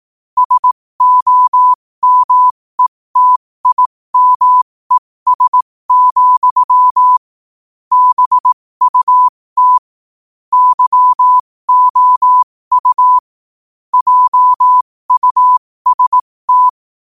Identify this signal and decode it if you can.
Morse code, heard as a keyed tone, about 18 wpm, spaced as for 15 wpm, 1000 Hz. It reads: SOMETIMES, BUT YOU JUST